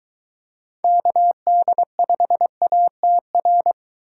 Morse code, heard as a keyed tone, about 23 wpm, 700 Hz. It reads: KD5ATR